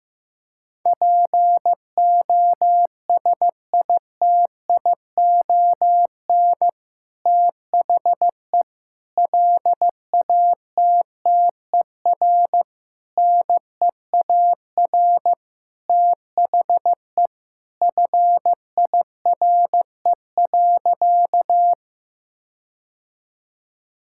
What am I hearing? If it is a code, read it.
Morse code, heard as a keyed tone, about 15 wpm, 700 Hz. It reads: POSITION THE LATTER NEAR THE FIRE.
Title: Sentences from Top 1600 Words 15wpm